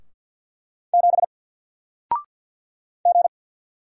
Morse code, heard as a keyed tone, about 48 wpm, 700 Hz. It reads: 7 G
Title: Single Letter-Number Alt 50wpm